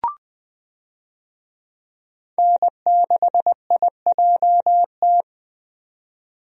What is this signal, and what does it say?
Morse code, heard as a keyed tone, about 20 wpm, 700 Hz. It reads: N6IJT
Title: USA Call Signs - 5 Characters - 20wpm